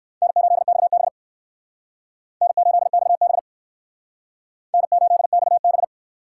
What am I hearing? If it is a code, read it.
Morse code, heard as a keyed tone, about 58 wpm, 700 Hz. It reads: N8XB N8XB N8XB